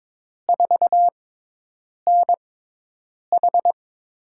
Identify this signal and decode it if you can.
Morse code, heard as a keyed tone, about 22 wpm, spaced as for 8 wpm, 700 Hz. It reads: 4NH
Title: USA Call Signs - 6x Character Spacing - 22wpm